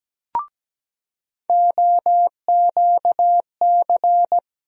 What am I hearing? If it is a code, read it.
Morse code, heard as a keyed tone, about 17 wpm, 700 Hz. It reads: OQC
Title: USA Call Sign Postfixes - 3 Letters 17wpm